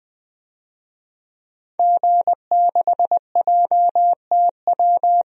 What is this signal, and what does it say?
Morse code, heard as a keyed tone, about 20 wpm, 700 Hz. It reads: G6JTW
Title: DXCC-100 Call Signs - 5 Characters - Repeated 3x - 20wpm